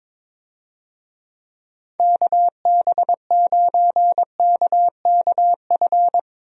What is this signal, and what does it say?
Morse code, heard as a keyed tone, about 22 wpm, 700 Hz. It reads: KB9KKF
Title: USA Call Signs - 6 Characters - Repeated 2x - 22wpm